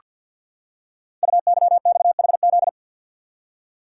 Morse code, heard as a keyed tone, about 48 wpm, 690 Hz. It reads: UYXHZ
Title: Code Groups - 5 Letters - 50wpm